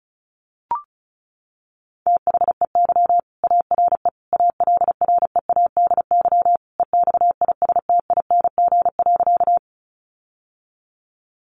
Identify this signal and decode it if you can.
Morse code, heard as a keyed tone, about 34 wpm, 700 Hz. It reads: THEY ARE ALREADY EXISTING.
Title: Sentences from Top 1500 Words 35wpm